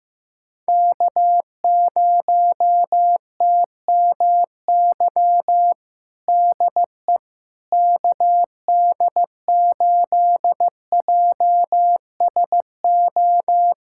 Morse code, heard as a keyed tone, about 15 wpm, 700 Hz. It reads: K0TMY DE KD8JSO
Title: USA Callsign DE Callsign 15wpm